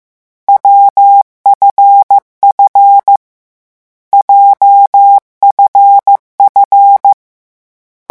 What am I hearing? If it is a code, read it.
Morse code, heard as a keyed tone, about 15 wpm, 800 Hz. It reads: WFF JFF
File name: morse mp3 / lecon7-vitesse 14